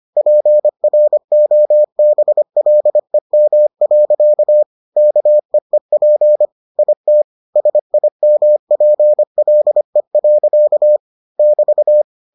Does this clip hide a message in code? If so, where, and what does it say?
Yes: Morse code from 0.2 to 12.0 s, heard as a keyed tone, about 25 wpm, 600 Hz. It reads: PROBLEM. KEEP IT SIMPLE. <BT>